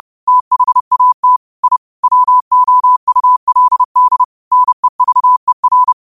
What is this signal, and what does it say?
Morse code, heard as a keyed tone, about 30 wpm, 1000 Hz. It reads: THAT I WOULD NEVER
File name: qotdfiles / 30wpm